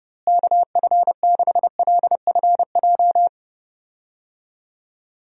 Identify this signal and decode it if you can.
Morse code, heard as a keyed tone, about 30 wpm, 700 Hz. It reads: KF6LFJ
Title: USA Call Signs - 6 Characters - Repeated 2x - 30wpm